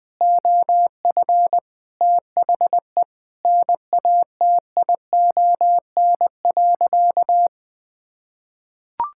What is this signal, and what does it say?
Morse code, heard as a keyed tone, about 21 wpm, 700 Hz. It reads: OF THE NATION.
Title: Sentences with 4-Word Phrases - Spoken Beginning - 20wpm